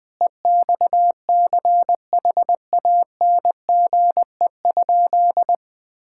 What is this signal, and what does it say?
Morse code, heard as a keyed tone, about 20 wpm, 700 Hz. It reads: EXCHANGE?
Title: Sentences from Top 1000 Words - Review Part 6 - 20wpm